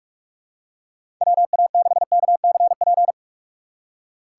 Morse code, heard as a keyed tone, about 47 wpm, 700 Hz. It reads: WA6KCP
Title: USA Call Signs - 6 Characters - 3x Character Spacing - 45wpm